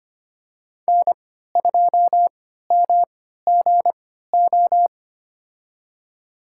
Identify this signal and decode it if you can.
Morse code, heard as a keyed tone, about 25 wpm, spaced as for 14 wpm, 700 Hz. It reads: N2MGO
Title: USA Call Signs - 5 Characters - 3x Character Spacing - 25wpm